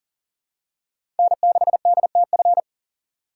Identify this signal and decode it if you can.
Morse code, heard as a keyed tone, about 40 wpm, 700 Hz. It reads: N6DTF
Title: USA Call Signs - 5 Characters - 4x Character Spacing - 40wpm